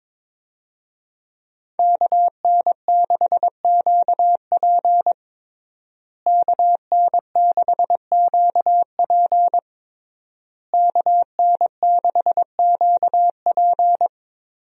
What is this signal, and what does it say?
Morse code, heard as a keyed tone, about 22 wpm, 700 Hz. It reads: KN6QP KN6QP KN6QP
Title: USA Call Signs - 5 Characters - Repeated 3x - 22wpm